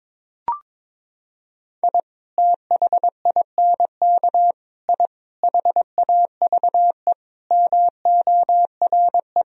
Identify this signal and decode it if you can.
Morse code, heard as a keyed tone, about 22 wpm, 700 Hz. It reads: I THINK I HAVE MORE
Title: Sentences from Top 1000 Words - Review Part 3 - 22wpm